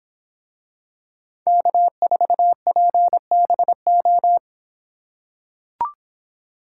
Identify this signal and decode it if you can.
Morse code, heard as a keyed tone, about 25 wpm, 700 Hz. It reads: K4PBO